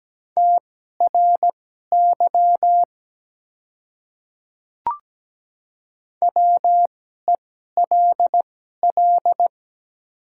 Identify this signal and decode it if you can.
Morse code, heard as a keyed tone, about 17 wpm, spaced as for 12 wpm, 700 Hz. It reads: TRY WELL
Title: 2x Character Spacing - Top 200 Words - 17wpm